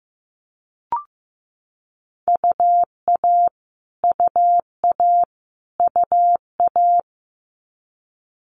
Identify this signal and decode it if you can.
Morse code, heard as a keyed tone, about 15 wpm, 700 Hz. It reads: UA UA UA
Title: Multi-letter Phonemes Repeated 3x 15wpm